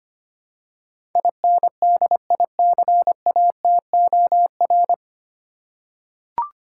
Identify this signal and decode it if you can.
Morse code, heard as a keyed tone, about 25 wpm, 700 Hz. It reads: INDICATOR
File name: Nine Letter Words - 25wpm